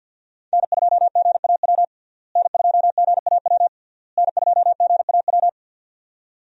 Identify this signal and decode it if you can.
Morse code, heard as a keyed tone, about 50 wpm, 700 Hz. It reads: N2GAW N2GAW N2GAW
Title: USA Call Signs - 5 Characters - Repeated 3x - 50wpm